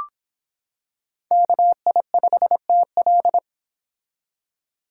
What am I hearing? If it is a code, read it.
Morse code, heard as a keyed tone, about 25 wpm, 700 Hz. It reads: KI5TL